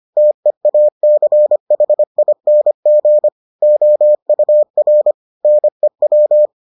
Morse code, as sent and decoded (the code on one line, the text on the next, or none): - . .- -.-. .... .. -. --. / --- ..- .-. / -. .--
TEACHING OUR NEW